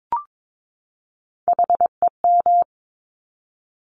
Morse code, heard as a keyed tone, about 22 wpm, 700 Hz. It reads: HEM
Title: Top Three Letter Combinations in Words 22wpm